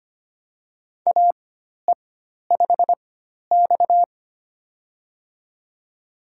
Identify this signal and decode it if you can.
Morse code, heard as a keyed tone, about 25 wpm, spaced as for 12 wpm, 700 Hz. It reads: AE5X
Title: USA Call Signs - 4x Character Spacing - 25wpm